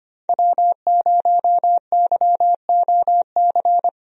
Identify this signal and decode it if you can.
Morse code, heard as a keyed tone, about 25 wpm, 700 Hz. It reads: W0YOC
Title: USA Call Signs - 5 Characters - 25wpm